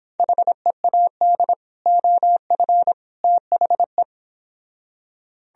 Morse code, heard as a keyed tone, about 26 wpm, 700 Hz. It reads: HEAD OF THE